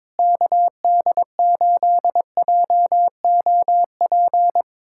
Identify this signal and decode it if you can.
Morse code, heard as a keyed tone, about 22 wpm, 700 Hz. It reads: KD8JOP
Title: USA Callsign DE Callsign - 6 Characters - 22wpm